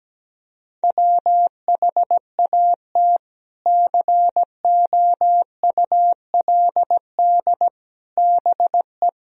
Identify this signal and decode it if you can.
Morse code, heard as a keyed tone, about 17 wpm, 700 Hz. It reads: WHAT COULD BE
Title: Sentences from Top 400 Words and Abbreviations 17wpm